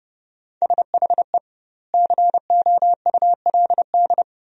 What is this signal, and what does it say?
Morse code, heard as a keyed tone, about 30 wpm, 700 Hz. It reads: SHE COULD
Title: 2x Word-Spacing - Sentences from Top 500 Words - 30wpm